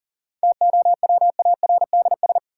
Morse code, heard as a keyed tone, about 39 wpm, 700 Hz. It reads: TOWARDS
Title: Sentences from Top 1100 Words - New Word Drill Alt - 40wpm